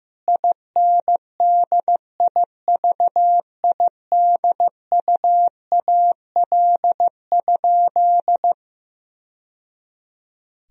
Morse code, heard as a keyed tone, about 15 wpm, 700 Hz. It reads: INDIVIDUAL?
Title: Sentences from Top 1000 Words - Review Part 4 - 15wpm